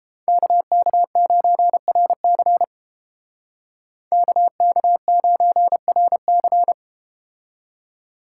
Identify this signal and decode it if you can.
Morse code, heard as a keyed tone, about 32 wpm, 700 Hz. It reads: KK9RC KK9RC